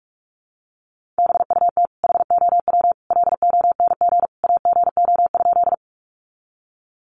Morse code, heard as a keyed tone, about 44 wpm, 700 Hz. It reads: BUT HOW LONG AGO?
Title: Sentences from Top 500 Words 45wpm